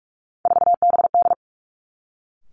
Morse code, heard as a keyed tone, about 46 wpm, 700 Hz. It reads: 4BD
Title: Three Challenging Letter-Number Combinations 45wpm